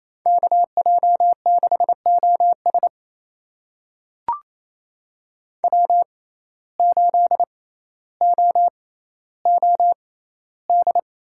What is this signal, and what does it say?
Morse code, heard as a keyed tone, about 28 wpm, spaced as for 14 wpm, 700 Hz. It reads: KJ6OS W8OOD
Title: USA Call Signs - 5 Characters - 6x Character Spacing - 28wpm